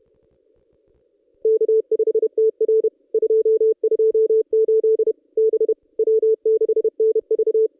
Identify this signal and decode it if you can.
Morse code, heard as a keyed tone, about 30 wpm, 440 Hz. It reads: K5TR 228 B W6NV
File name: contests / ss-cw-k5tr-op-n5rz -1800